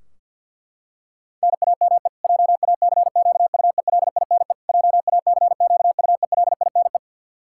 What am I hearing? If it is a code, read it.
Morse code, heard as a keyed tone, about 50 wpm, 700 Hz. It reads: NAME JACQUELINE JACQUELINE
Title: Practice QSO - Exchange 1 - Beginning - 50wpm